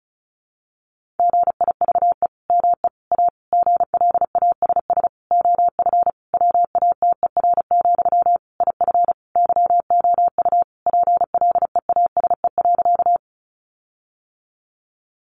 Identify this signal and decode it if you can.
Morse code, heard as a keyed tone, about 34 wpm, 700 Hz. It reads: GIVE ME A GLASS OF WATER, IF YOU PLEASE.
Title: Sentences from Top 1500 Words 35wpm